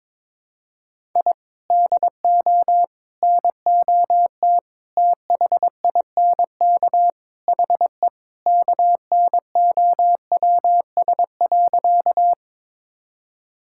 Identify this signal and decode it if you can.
Morse code, heard as a keyed tone, about 22 wpm, 700 Hz. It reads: I DO NOT THINK HE KNOWS.
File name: practice / 22wpm / Sentences with 4-Word Phrases - 22wpm